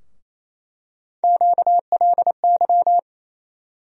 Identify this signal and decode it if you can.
Morse code, heard as a keyed tone, about 27 wpm, 700 Hz. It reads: QLY
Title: USA Call Sign Postfixes - 3 Letters 28wpm